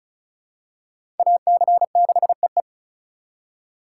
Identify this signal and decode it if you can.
Morse code, heard as a keyed tone, about 36 wpm, 700 Hz. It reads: AC6EE
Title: USA Call Signs - 3x Character Spacing - 35wpm